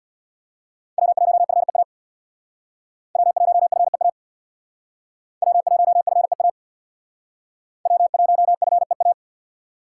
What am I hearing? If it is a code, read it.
Morse code, heard as a keyed tone, about 61 wpm, 700 Hz. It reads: W1FEA W1FEA W1FEA W1FEA
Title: Speed Racing - USA 1x3 Call Signs 50wpm